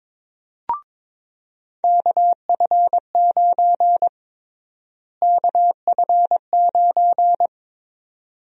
Morse code, heard as a keyed tone, about 22 wpm, 700 Hz. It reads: KF9 KF9
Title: USA Call Sign Prefixes - Repeated 2x - 22wpm